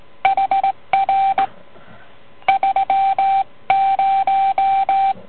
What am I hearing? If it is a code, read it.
Morse code, heard as a keyed tone, about 19 wpm, 780 Hz. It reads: HR 30